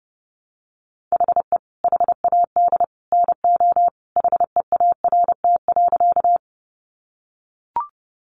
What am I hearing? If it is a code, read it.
Morse code, heard as a keyed tone, about 30 wpm, 700 Hz. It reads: HE HAD NO HEART.
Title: Sentences from Top 1200 Words 30wpm